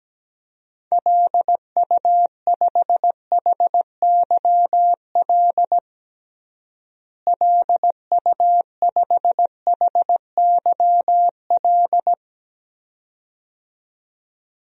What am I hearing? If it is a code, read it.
Morse code, heard as a keyed tone, about 17 wpm, 700 Hz. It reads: LU5HYL LU5HYL
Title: DXCC-100 Call Signs - 6 Characters - Repeated 2x - 17wpm